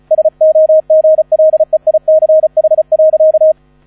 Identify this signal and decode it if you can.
Morse code, heard as a keyed tone, about 34 wpm, 630 Hz, with radio noise. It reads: SOGLEICH.
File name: 35wpm